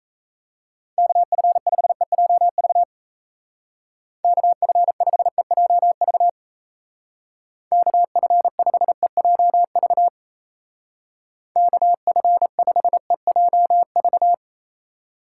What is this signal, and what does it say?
Morse code, heard as a keyed tone, about 34 wpm, 700 Hz. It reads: KF5EJV KF5EJV KF5EJV KF5EJV
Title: Speed Racing - USA 2x3 Call Signs 28wpm